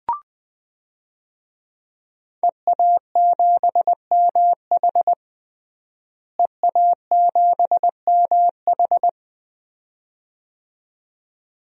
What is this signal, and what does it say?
Morse code, heard as a keyed tone, about 20 wpm, 700 Hz. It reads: EA7MH EA7MH
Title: DXCC-100 Call Signs - 5 Characters - Repeated 2x - 20wpm